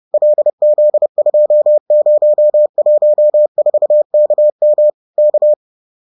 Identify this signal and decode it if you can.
Morse code, heard as a keyed tone, about 30 wpm, 600 Hz. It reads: LZ2014KM K